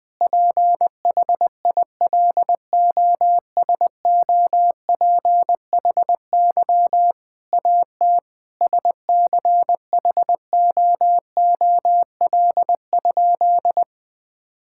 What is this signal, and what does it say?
Morse code, heard as a keyed tone, about 21 wpm, 700 Hz. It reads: PHILOSOPHY AT SCHOOL?
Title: Sentences from Top 1000 Words - Review Part 6 - 20wpm